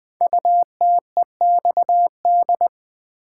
Morse code, heard as a keyed tone, about 20 wpm, 700 Hz. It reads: UTEXD
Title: Code Groups - 5 Letters - 20wpm